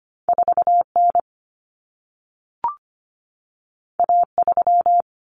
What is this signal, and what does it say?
Morse code, heard as a keyed tone, about 25 wpm, 700 Hz. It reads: 4N A3
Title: Common Cut Numbers - 2-Digit 25wpm